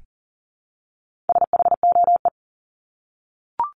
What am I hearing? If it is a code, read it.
Morse code, heard as a keyed tone, about 40 wpm, 700 Hz. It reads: SHOE
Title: Top 1751-2000 Words 40wpm